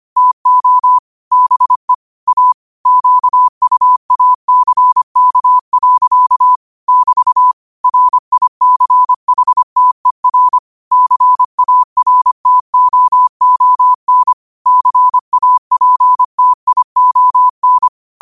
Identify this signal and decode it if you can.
Morse code, heard as a keyed tone, about 25 wpm, 1000 Hz. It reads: TO BE A QUACK. <BT> RICHTER CARTOON CAPTION
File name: qotdfiles / 25wpm